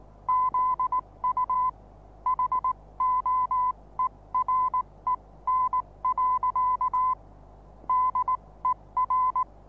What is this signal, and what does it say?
Morse code, heard as a keyed tone, about 19 wpm, spaced as for 16 wpm, 1010 Hz. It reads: ZU HOEREN. DER
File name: repos / Morsekurs / DJ4UF Morsen26